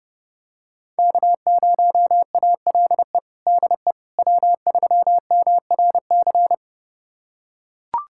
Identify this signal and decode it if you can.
Morse code, heard as a keyed tone, about 29 wpm, 700 Hz. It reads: K0ALE DE W3MRC
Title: USA Callsign DE Callsign - 5 Characters - 30wpm